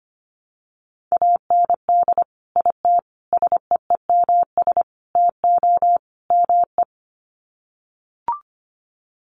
Morse code, heard as a keyed tone, about 25 wpm, 700 Hz. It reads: AND IT SEEMS TO ME
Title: Sets of 5 Words Encore2 25wpm